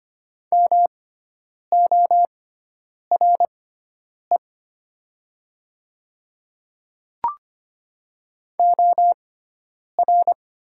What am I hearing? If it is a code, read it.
Morse code, heard as a keyed tone, about 25 wpm, spaced as for 9 wpm, 690 Hz. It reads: MORE OR
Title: 6x Character Spacing - Top 100 Words - 25wpm